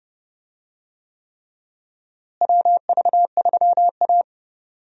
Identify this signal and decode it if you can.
Morse code, heard as a keyed tone, about 30 wpm, 700 Hz. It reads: WV3A
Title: USA Call Signs - 4 Characters - Repeated 3x - 30wpm